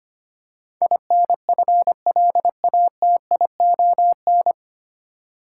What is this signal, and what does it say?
Morse code, heard as a keyed tone, about 25 wpm, 700 Hz. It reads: INFLATION
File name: practice / 25wpm / 9 Letter Words - First 3 Letters Given - 25wpm